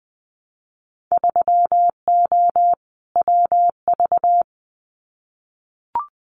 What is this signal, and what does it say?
Morse code, heard as a keyed tone, about 20 wpm, 700 Hz. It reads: 3O WV
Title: Winter Field Day Exchanges 20wpm